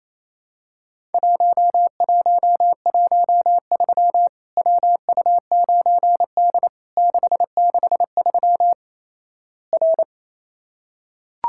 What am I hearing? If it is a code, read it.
Morse code, heard as a keyed tone, about 28 wpm, 700 Hz. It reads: 1113 WU9D 663 R
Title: Worked All Europe DX Contest - QTC Exchange - 28wpm